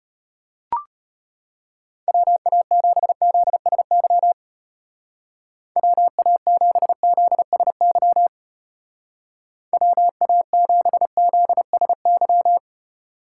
Morse code, heard as a keyed tone, about 30 wpm, 700 Hz. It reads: WA7ZSY WA7ZSY WA7ZSY
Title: Speed Racing - USA 2x3 Call Signs 25wpm